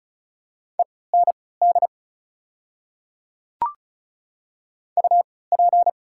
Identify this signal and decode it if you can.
Morse code, heard as a keyed tone, about 36 wpm, spaced as for 20 wpm, 690 Hz. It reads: END UP
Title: 3x Character Spacing - Top 200 Words - 35wpm